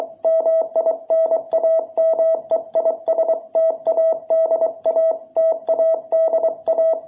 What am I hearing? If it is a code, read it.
Morse code, heard as a keyed tone, about 23 wpm, 660 Hz. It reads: MI NAME IS TADA TADA
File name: jh8cbh / 2021cw